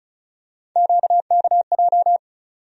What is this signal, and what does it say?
Morse code, heard as a keyed tone, about 36 wpm, 700 Hz. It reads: QKJ
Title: USA Call Sign Postfixes - 3 Letters 35wpm